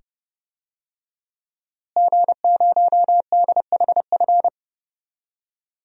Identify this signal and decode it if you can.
Morse code, heard as a keyed tone, about 30 wpm, 700 Hz. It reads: G0DHF
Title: DXCC-100 Call Signs - 5 Characters - Repeated 2x - 30wpm